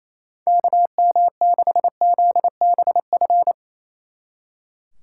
Morse code, heard as a keyed tone, about 34 wpm, 700 Hz. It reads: KM6ZBF
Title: Speed Racing - USA 2x3 Call Signs 28wpm